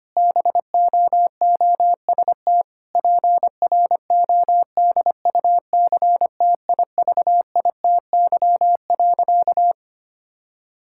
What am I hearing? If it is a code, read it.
Morse code, heard as a keyed tone, about 25 wpm, 700 Hz. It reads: BOOST PRODUCTIVITY.
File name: practice / 25wpm / Difficult Sentences with Hints - 25wpm